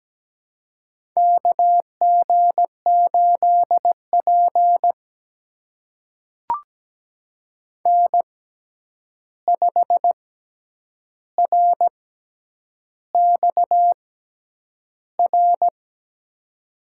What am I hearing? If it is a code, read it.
Morse code, heard as a keyed tone, about 17 wpm, spaced as for 8 wpm, 700 Hz. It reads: KG8P N5RXR
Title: USA Call Signs - 6x Character Spacing - 17wpm